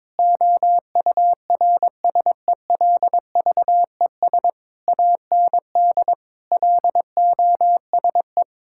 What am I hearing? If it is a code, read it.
Morse code, heard as a keyed tone, about 22 wpm, 700 Hz. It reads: OURSELVES AND LOSE
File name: Difficult Sentences with Hints - 22wpm